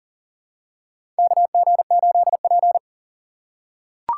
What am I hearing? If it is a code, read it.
Morse code, heard as a keyed tone, about 40 wpm, 700 Hz. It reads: KG8P
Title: USA Call Signs - 4 Characters - 6x Character Spacing - 40wpm